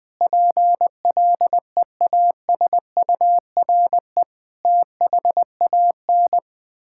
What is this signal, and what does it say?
Morse code, heard as a keyed tone, about 20 wpm, 700 Hz. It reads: PLEASURE THAN
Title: Sentences from Top 1600 Words 20wpm